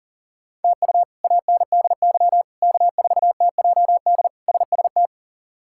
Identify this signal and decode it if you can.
Morse code, heard as a keyed tone, about 40 wpm, 700 Hz. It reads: TU ANDY K4TJD SST
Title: K1USN SST QSO Practice - 40wpm